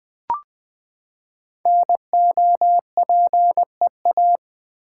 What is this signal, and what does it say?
Morse code, heard as a keyed tone, about 20 wpm, 700 Hz. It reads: NOPEA
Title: Code Groups - 5 Letters - 20wpm